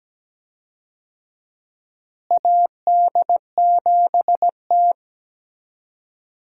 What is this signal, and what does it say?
Morse code, heard as a keyed tone, about 17 wpm, 700 Hz. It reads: AD7T
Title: USA Call Signs - 4 Characters - Repeated 2x - 17wpm